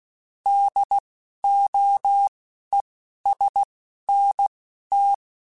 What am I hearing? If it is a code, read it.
Morse code, heard as a keyed tone, about 16 wpm, spaced as for 11 wpm, 800 Hz. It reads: DOESNT
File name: QSOs / QSOs-mp3s / WD0BGP 2@10